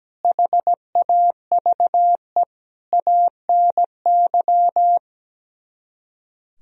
Morse code, heard as a keyed tone, about 17 wpm, 700 Hz. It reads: HAVE ANY
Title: Speed-Racing - Sets of 5 Words 17wpm